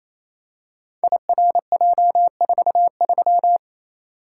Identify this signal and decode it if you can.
Morse code, heard as a keyed tone, about 28 wpm, 700 Hz. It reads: IRJ43
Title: Code Groups - 5 Letter:Numbers - 28wpm